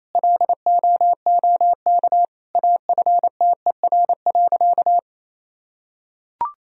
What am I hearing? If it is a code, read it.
Morse code, heard as a keyed tone, about 27 wpm, 700 Hz. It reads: LOOK AFTER.
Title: Sentences from Top 1000 Words - Review Part 3 - 28wpm